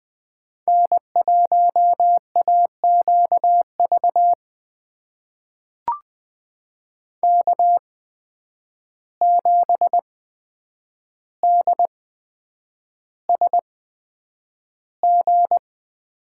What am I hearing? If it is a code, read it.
Morse code, heard as a keyed tone, about 20 wpm, spaced as for 9 wpm, 700 Hz. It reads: N1AQV K7DSG
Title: USA Call Signs - 5 Characters - 8x Character Spacing - 20wpm